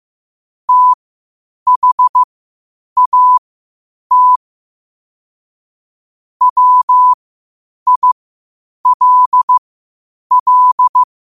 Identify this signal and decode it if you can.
Morse code, heard as a keyed tone, about 15 wpm, spaced as for 8 wpm, 1000 Hz. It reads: THAT WILL
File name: qotdfiles / 8wpm